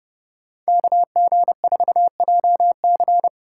Morse code, heard as a keyed tone, about 30 wpm, 700 Hz. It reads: KG4JC